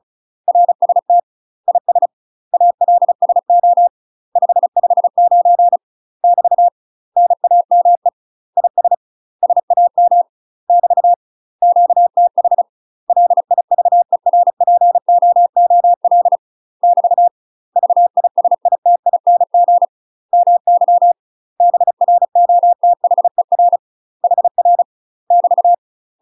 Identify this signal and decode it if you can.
Morse code, heard as a keyed tone, about 34 wpm, 700 Hz. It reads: RST IS ALSO 559 <BT> NAME IS SAM <BT> QTH LIVERPOOL <BT> VISITING MY BROTHER HR <BT>
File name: UK MASTER QSOs TRAINING / 35WPM / UK MASTER QSOs 0025